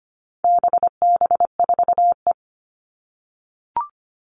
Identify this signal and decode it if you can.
Morse code, heard as a keyed tone, about 25 wpm, 700 Hz. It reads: BB4E
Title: All Cut Numbers - 4-Digit 25wpm